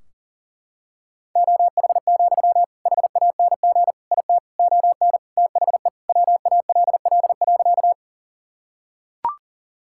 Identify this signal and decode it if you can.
Morse code, heard as a keyed tone, about 37 wpm, 700 Hz. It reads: OH, HANG IT ON THE WALL.
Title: Sentences from Top 1600 Words 40wpm